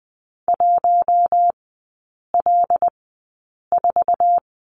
Morse code, heard as a keyed tone, about 20 wpm, spaced as for 8 wpm, 700 Hz. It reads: 1L4
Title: 133 Lesson 22e - Question Mark - Three Characters 20wpm